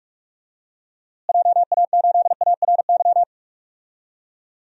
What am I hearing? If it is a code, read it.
Morse code, heard as a keyed tone, about 46 wpm, 700 Hz. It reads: JA8ARY
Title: DXCC-100 Call Signs - 6 Characters - Repeated 2x - 45wpm